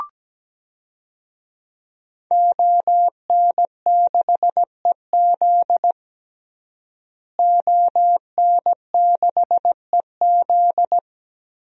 Morse code, heard as a keyed tone, about 17 wpm, 700 Hz. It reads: ON6EZ ON6EZ
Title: DXCC-100 Call Signs - 5 Characters - Repeated 3x - 17wpm